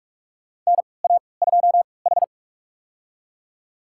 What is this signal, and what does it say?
Morse code, heard as a keyed tone, about 42 wpm, spaced as for 25 wpm, 700 Hz. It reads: NA2H
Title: USA Call Signs - 3x Character Spacing - 45wpm